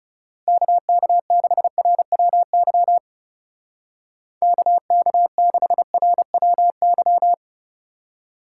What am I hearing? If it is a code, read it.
Morse code, heard as a keyed tone, about 37 wpm, 700 Hz. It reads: KK6RWY KK6RWY
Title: Speed Racing - USA Call Signs 30wpm